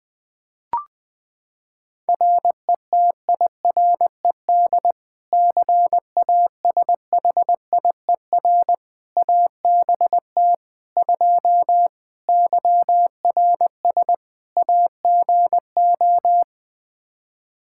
Morse code, heard as a keyed tone, about 20 wpm, 700 Hz. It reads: RETIRED CASHIER ABT 2 YRS AGO
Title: K1USN SST QSO Practice 20wpm